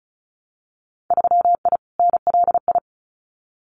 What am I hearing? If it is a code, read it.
Morse code, heard as a keyed tone, about 34 wpm, 700 Hz. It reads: 3I NLI
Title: Winter Field Day Exchanges Encore 35wpm